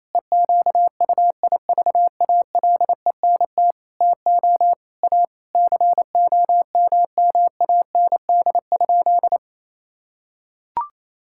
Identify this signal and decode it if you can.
Morse code, heard as a keyed tone, about 28 wpm, 700 Hz. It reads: EQUIVALENT TO A COMMAND?
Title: Sentences from Top 1600 Words 28wpm